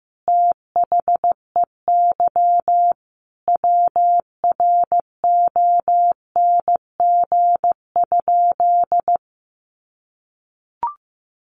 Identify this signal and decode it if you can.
Morse code, heard as a keyed tone, about 15 wpm, 700 Hz. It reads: THEY WRONG?
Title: Sentences from Top 900 Words 15wpm